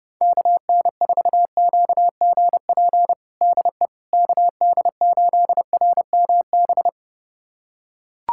Morse code, heard as a keyed tone, about 30 wpm, 700 Hz. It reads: KN4QGP DE KD8RMB
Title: USA Callsign DE Callsign 30wpm